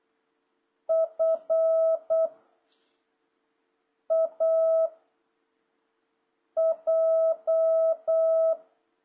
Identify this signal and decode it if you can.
Morse code, heard as a keyed tone, about 8 wpm, 660 Hz. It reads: F A J
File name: jh8cbh / 2026cw